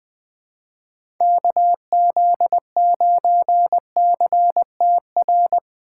Morse code, heard as a keyed tone, about 20 wpm, 700 Hz. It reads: KZ9CTR
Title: USA Call Signs - 6x Character Spacing - 20wpm